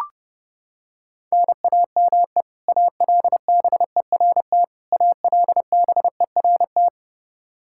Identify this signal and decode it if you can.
Morse code, heard as a keyed tone, about 30 wpm, 700 Hz. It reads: NAME ALBERT ALBERT
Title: Practice QSO - Exchange 1 - Beginning - 30wpm